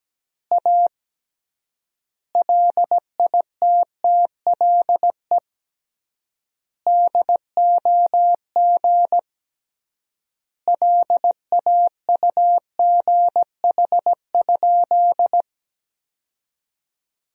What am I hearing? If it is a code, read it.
Morse code, heard as a keyed tone, about 17 wpm, 700 Hz. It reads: A LITTLE DOG LAUGH?
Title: 3x Word-Spacing - Sentences from Top 500 Words - 17wpm